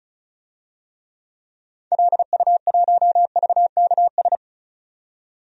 Morse code, heard as a keyed tone, about 36 wpm, 700 Hz. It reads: LU1VKS
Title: DXCC-100 Call Signs - 6 Characters - 35wpm